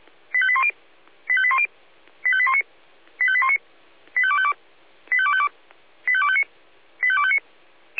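Five-tone selective call (ZVEI), heard as quick runs of five tones, tones of about 70 ms. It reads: 85810 85810 85819 85819 85252 85252 85259 85259
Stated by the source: ZVEI five-tone selective call received by SDR.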